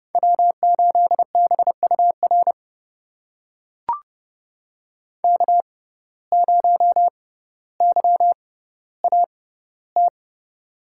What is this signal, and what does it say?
Morse code, heard as a keyed tone, about 30 wpm, spaced as for 15 wpm, 700 Hz. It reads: W8BUR K0YAT
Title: USA Call Signs - 5 Characters - 6x Character Spacing - 30wpm